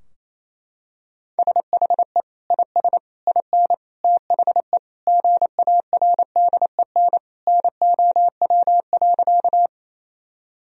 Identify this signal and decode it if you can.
Morse code, heard as a keyed tone, about 28 wpm, 700 Hz. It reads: SHE IS IN THE GARDEN NOW.
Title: Sentences from Top 1000 Words - Review Part 4 - 28wpm